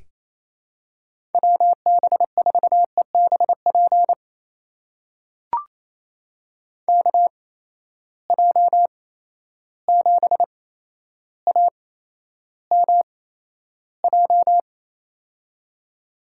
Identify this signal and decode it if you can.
Morse code, heard as a keyed tone, about 28 wpm, spaced as for 12 wpm, 700 Hz. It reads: WB4EBP KJ7AMJ
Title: USA Call Signs - 6 Characters - 8x Character Spacing - 28wpm